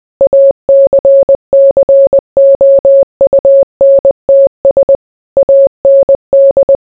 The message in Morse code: ACCOUNTS AND

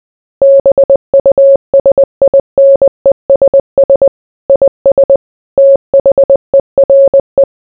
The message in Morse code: BUSINESS IS THERE

